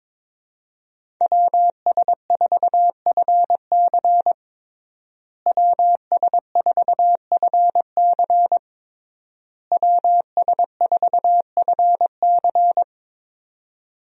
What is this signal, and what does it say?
Morse code, heard as a keyed tone, about 22 wpm, 700 Hz. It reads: WS4FC WS4FC WS4FC
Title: USA Call Signs - 5 Characters - Repeated 3x - 22wpm